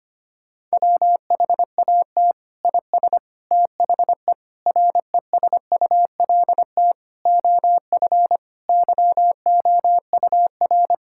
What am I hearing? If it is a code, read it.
Morse code, heard as a keyed tone, about 25 wpm, 700 Hz. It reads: WHAT IS THE RESULT OF YOUR
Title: Sentences from Top 1000 Words - Review Part 5 - 25wpm